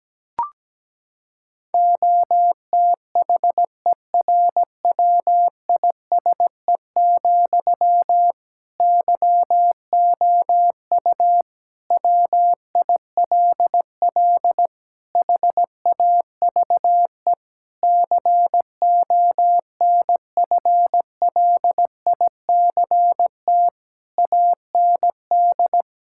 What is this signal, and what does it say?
Morse code, heard as a keyed tone, about 17 wpm, 700 Hz. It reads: OTHERWISE, YOU WILL HAVE CONFLICT AND
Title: Sentences from Top 1600 Words 17wpm